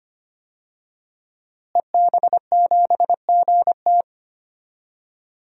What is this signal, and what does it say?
Morse code, heard as a keyed tone, about 25 wpm, 700 Hz. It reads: EB7GT
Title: DXCC-100 Call Signs - 5 Characters - Repeated 2x - 25wpm